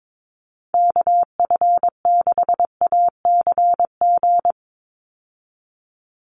Morse code, heard as a keyed tone, about 22 wpm, 700 Hz. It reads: KF6ACG
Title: USA Call Signs 22wpm